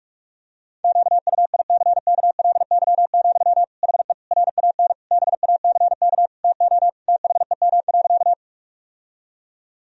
Morse code, heard as a keyed tone, about 42 wpm, 700 Hz. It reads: QUICKLY, HE RAN BACK TO THEM.
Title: Sentences from Top 1000 Words - Review Part 3 - 45wpm